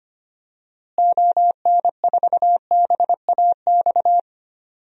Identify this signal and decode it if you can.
Morse code, heard as a keyed tone, about 25 wpm, 700 Hz. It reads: ON4BAX
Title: DXCC-100 Call Signs - 6 Characters - Repeated 2x - 25wpm